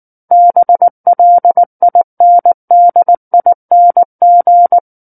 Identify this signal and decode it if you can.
Morse code, heard as a keyed tone, about 19 wpm, 700 Hz. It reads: BLINDING
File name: Book One / 10-Chapter 10 - 19 WPM